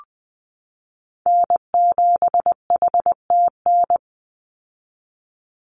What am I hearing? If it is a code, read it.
Morse code, heard as a keyed tone, about 20 wpm, 700 Hz. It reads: N7HTN
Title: USA Call Signs 20wpm